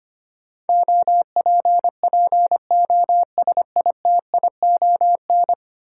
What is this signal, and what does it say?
Morse code, heard as a keyed tone, about 25 wpm, 700 Hz. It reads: OPPOSITION
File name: Ten Letter Words - 25wpm